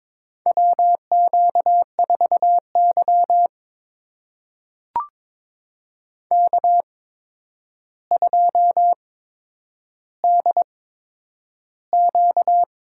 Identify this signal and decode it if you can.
Morse code, heard as a keyed tone, about 22 wpm, spaced as for 9 wpm, 700 Hz. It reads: WQ4Y K2DQ
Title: USA Call Signs - 4 Characters - 8x Character Spacing - 22wpm